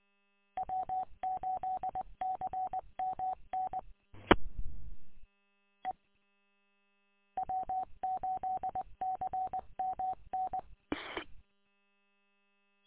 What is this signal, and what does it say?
Morse code, heard as a keyed tone, about 20 wpm, 750 Hz. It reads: W8CMN E E W8CMN
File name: w8cmn- -combined